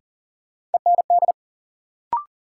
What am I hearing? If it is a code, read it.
Morse code, heard as a keyed tone, about 40 wpm, 700 Hz. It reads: END
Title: Sentences from Top 200 Words - New Word Drill - 40wpm